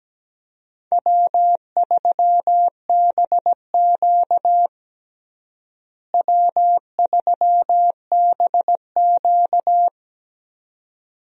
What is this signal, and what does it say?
Morse code, heard as a keyed tone, about 17 wpm, 700 Hz. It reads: W3BQ W3BQ
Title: USA Call Signs - 4 Characters - Repeated 2x - 17wpm